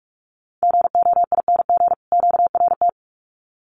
Morse code, heard as a keyed tone, about 44 wpm, 700 Hz. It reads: GOING QRT
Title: Common QSO Phrases 45wpm